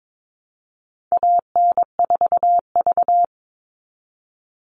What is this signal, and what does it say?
Morse code, heard as a keyed tone, about 22 wpm, 700 Hz. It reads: AN4V
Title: All Cut Numbers - 4-Digit 22wpm